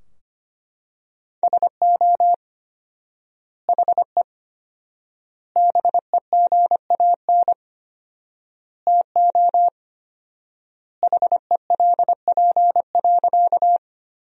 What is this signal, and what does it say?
Morse code, heard as a keyed tone, about 25 wpm, 700 Hz. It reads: SO HE BEGAN TO HELP.
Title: 4x Word-Spacing - Sentences from Top 300 Words - 25wpm